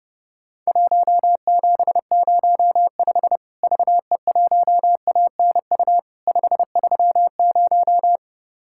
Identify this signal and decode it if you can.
Morse code, heard as a keyed tone, about 30 wpm, 700 Hz. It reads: 1705 VE1ANU 530
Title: Worked All Europe DX Contest - QTC Exchange - 30wpm